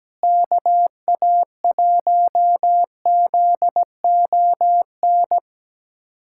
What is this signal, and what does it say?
Morse code, heard as a keyed tone, about 17 wpm, 700 Hz. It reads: KA1ZON
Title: USA Callsign DE Callsign - 6 Characters - 17wpm